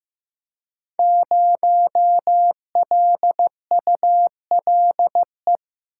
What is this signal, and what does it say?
Morse code, heard as a keyed tone, about 15 wpm, 700 Hz. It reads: 0LULE
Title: Code Groups - 5 Letter:Numbers - 15wpm